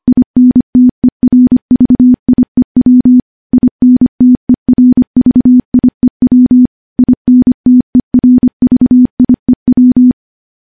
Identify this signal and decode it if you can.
Morse code, heard as a keyed tone, about 25 wpm, 250 Hz. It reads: INTERVIEW INTERVIEW INTERVIEW
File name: morse / morse-training-25WPM